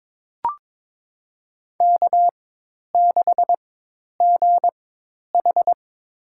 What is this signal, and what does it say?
Morse code, heard as a keyed tone, about 22 wpm, spaced as for 10 wpm, 700 Hz. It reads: K6GH
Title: USA Call Signs - 4x Character Spacing - 22wpm